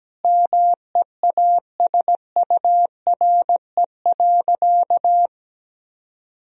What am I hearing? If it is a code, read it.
Morse code, heard as a keyed tone, about 17 wpm, 700 Hz. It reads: MEASURE.
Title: Sentences from Top 1000 Words - Review Part 2 - 17wpm